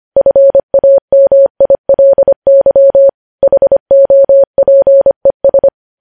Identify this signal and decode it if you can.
Morse code, heard as a keyed tone, about 25 wpm, 550 Hz. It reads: FAMILY HOPES
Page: News Headlines in Morse Code at 25 WPM